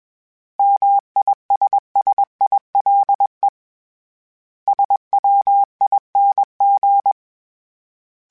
Morse code, heard as a keyed tone, about 21 wpm, 800 Hz. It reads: MISSILE SWING